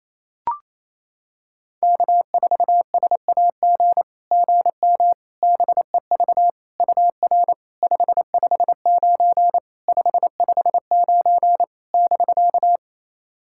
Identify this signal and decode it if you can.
Morse code, heard as a keyed tone, about 28 wpm, 700 Hz. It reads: K4SAG GM BEV UR 559 559 <BK>
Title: SOTA QSO Practice 28wpm